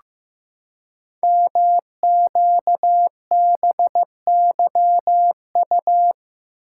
Morse code, heard as a keyed tone, about 15 wpm, 700 Hz. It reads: MQBYU
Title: Code Groups - 5 Letters - 15wpm